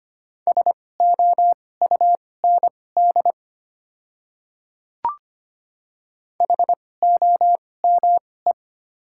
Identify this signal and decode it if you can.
Morse code, heard as a keyed tone, about 25 wpm, spaced as for 18 wpm, 700 Hz. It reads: SOUND HOME
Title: 2x Character Spacing - Top 300 Words - 25wpm